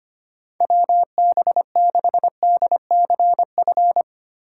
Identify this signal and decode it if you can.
Morse code, heard as a keyed tone, about 25 wpm, 700 Hz. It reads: WB6DCF
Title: USA Call Signs - 6 Characters - 2x Character Spacing - 25wpm